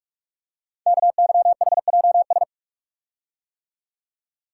Morse code, heard as a keyed tone, about 44 wpm, 700 Hz. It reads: KYHJS
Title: Code Groups - 5 Letters - 45wpm